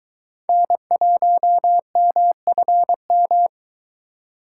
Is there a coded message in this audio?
Yes: Morse code from 0.5 to 3.5 s, heard as a keyed tone, about 22 wpm, 700 Hz. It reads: N1MFM